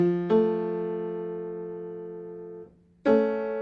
Dial tone (North America), 350 + 440 Hz.